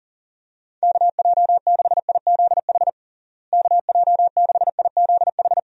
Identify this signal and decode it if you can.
Morse code, heard as a keyed tone, about 40 wpm, 700 Hz. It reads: KJ6IZH KJ6IZH
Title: USA Call Signs - 6 Characters - Repeated 2x - 40wpm